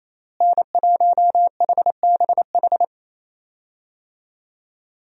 Morse code, heard as a keyed tone, about 28 wpm, 700 Hz. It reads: N1HBH
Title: USA Call Signs - 5 Characters - Repeated 2x - 28wpm